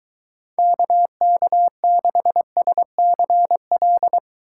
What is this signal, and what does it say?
Morse code, heard as a keyed tone, about 23 wpm, 700 Hz. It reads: KK6SCL